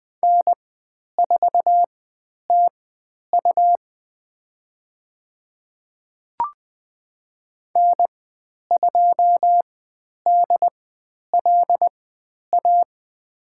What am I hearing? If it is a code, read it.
Morse code, heard as a keyed tone, about 20 wpm, spaced as for 10 wpm, 700 Hz. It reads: N4TU N2DLA
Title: 102 Lesson 17g - Letter u - Call Signs 20wpm